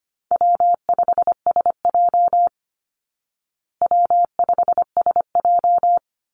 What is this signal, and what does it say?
Morse code, heard as a keyed tone, about 25 wpm, 700 Hz. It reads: W5SJ W5SJ
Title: Call Signs Repeated 2x - 25wpm